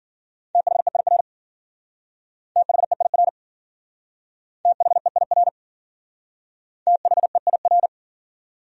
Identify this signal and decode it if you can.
Morse code, heard as a keyed tone, about 50 wpm, 700 Hz. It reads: THEIR THEIR THEIR THEIR
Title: Speed-Racing Top 100 Words 40wpm